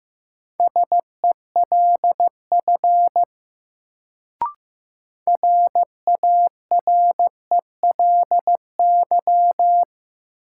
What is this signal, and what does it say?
Morse code, heard as a keyed tone, about 15 wpm, 700 Hz. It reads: SELF RARELY
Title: Sentences from Top 1400 Words - New Word Drill Alt - 15wpm